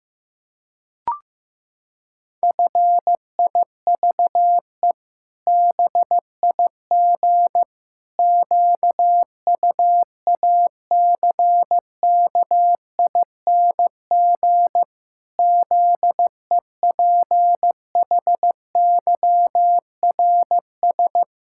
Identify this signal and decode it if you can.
Morse code, heard as a keyed tone, about 15 wpm, 700 Hz. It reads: FIVE BIG QUACKING ZEPHYRS
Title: Pangram Challenge v2 - 15wpm